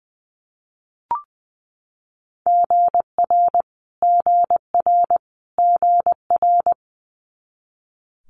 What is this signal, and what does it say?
Morse code, heard as a keyed tone, about 20 wpm, 700 Hz. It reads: GR GR GR
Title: Multi-letter Phonemes Repeated 3x 20wpm